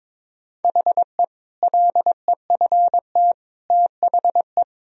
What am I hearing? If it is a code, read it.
Morse code, heard as a keyed tone, about 22 wpm, 700 Hz. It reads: HE LEFT THE
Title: Sentences from Top 1000 Words - Review Part 4 - 22wpm